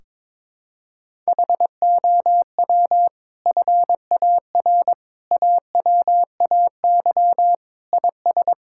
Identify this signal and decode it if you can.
Morse code, heard as a keyed tone, about 22 wpm, 700 Hz. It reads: HOW FAR AWAY IS
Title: Sentences from Top 1600 Words 22wpm